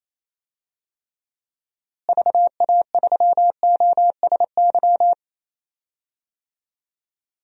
Morse code, heard as a keyed tone, about 28 wpm, 700 Hz. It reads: VA3OSY
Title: DXCC-100 Call Signs - 6 Characters - Repeated 2x - 28wpm